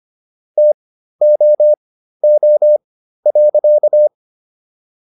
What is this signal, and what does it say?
Morse code, heard as a keyed tone, about 25 wpm, spaced as for 13 wpm, 600 Hz. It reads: TOO.